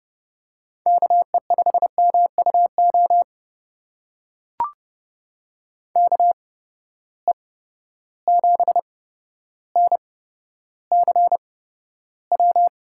Morse code, heard as a keyed tone, about 30 wpm, spaced as for 13 wpm, 700 Hz. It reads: KE5MUO KE7NCW
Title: USA Call Signs - 6 Characters - 8x Character Spacing - 30wpm